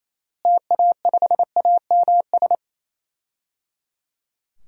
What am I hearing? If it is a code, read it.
Morse code, heard as a keyed tone, about 28 wpm, 700 Hz. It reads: TA5AMS
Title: DXCC-100 Call Signs - 6 Characters - Repeated 3x - 28wpm